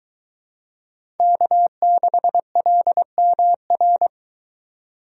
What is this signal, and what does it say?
Morse code, heard as a keyed tone, about 21 wpm, 700 Hz. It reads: K6LMR